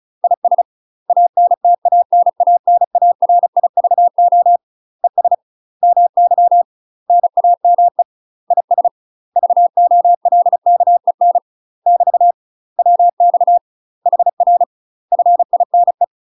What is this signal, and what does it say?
Morse code, heard as a keyed tone, about 36 wpm, 700 Hz. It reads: IS ANTANANARIVO ES MY NAME IS VOLKEN <BT> WX HR FINE
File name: UK MASTER QSOs TRAINING / 35WPM / UK MASTER QSOs 0014